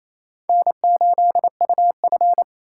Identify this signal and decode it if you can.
Morse code, heard as a keyed tone, about 28 wpm, 700 Hz. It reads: N8UF
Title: USA Call Signs - 4 Characters - 6x Character Spacing - 28wpm